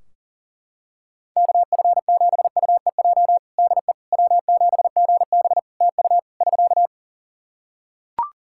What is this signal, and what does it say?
Morse code, heard as a keyed tone, about 40 wpm, 700 Hz. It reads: KF7UEJ DE W7GB TU <SK>
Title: Practice QSO - Exchange 4 - Ending - 40wpm